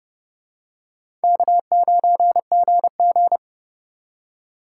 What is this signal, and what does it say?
Morse code, heard as a keyed tone, about 26 wpm, 690 Hz. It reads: K9GG